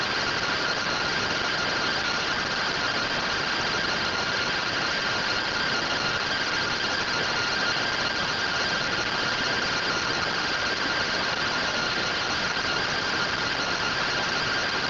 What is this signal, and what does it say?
Two-tone teleprinter signal, about 170 Hz shift.